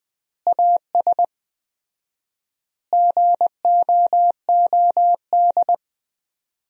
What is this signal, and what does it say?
Morse code, heard as a keyed tone, about 20 wpm, 700 Hz. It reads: AS GOOD
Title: 4x Word-Spacing - Sentences from Top 200 Words - 20wpm